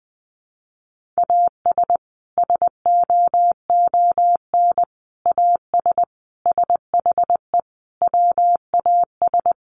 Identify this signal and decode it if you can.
Morse code, heard as a keyed tone, about 20 wpm, 700 Hz. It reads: AS SOON AS SHE WAS
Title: Sentences from Top 1100 Words 20wpm